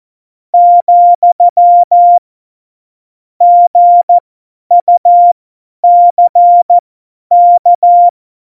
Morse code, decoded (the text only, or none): , GUCK